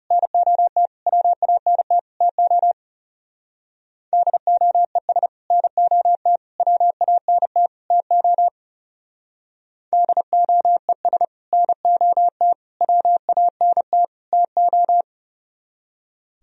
Morse code, heard as a keyed tone, about 36 wpm, 700 Hz. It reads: NOT WANT TO DOES NOT WANT TO DOES NOT WANT TO
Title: Speed-Racing - Sets of 4 Words 30wpm